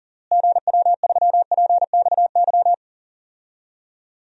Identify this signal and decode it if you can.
Morse code, heard as a keyed tone, about 40 wpm, 700 Hz. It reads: GW3PXY
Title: DXCC-100 Call Signs - 6 Characters - Repeated 3x - 40wpm